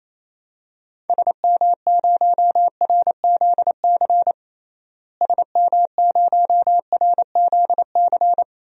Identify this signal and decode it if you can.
Morse code, heard as a keyed tone, about 28 wpm, 700 Hz. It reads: SM0RZC SM0RZC
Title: DXCC-100 Call Signs - 6 Characters - Repeated 2x - 28wpm